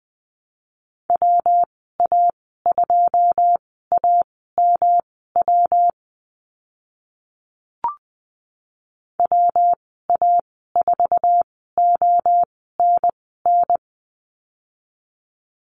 Morse code, heard as a keyed tone, about 20 wpm, spaced as for 14 wpm, 700 Hz. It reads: WA2AMW WA4ONN
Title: 161 Lesson 26g - Letter g - Call Signs 20wpm